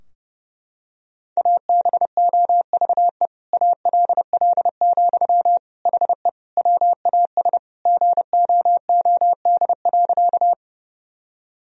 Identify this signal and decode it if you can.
Morse code, heard as a keyed tone, about 32 wpm, 700 Hz. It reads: ABOVE ALL, HE WAS GOOD.
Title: Sentences from Top 1000 Words - Review Part 2 - 30wpm